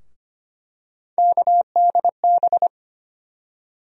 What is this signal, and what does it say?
Morse code, heard as a keyed tone, about 25 wpm, 700 Hz. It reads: KDB
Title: USA Call Sign Postfixes - 3 Letters 25wpm